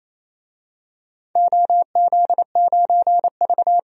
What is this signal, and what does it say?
Morse code, heard as a keyed tone, about 27 wpm, 700 Hz. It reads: OZ9V
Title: DXCC-100 Call Signs - 4 Characters - 28wpm